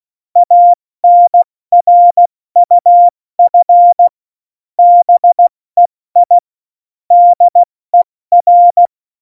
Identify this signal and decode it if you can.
Morse code, heard as a keyed tone, about 16 wpm, 700 Hz. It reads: ANRUF BEI DER
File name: Morsekurs / Witze / Witz 28. Januar 2020, 70 F80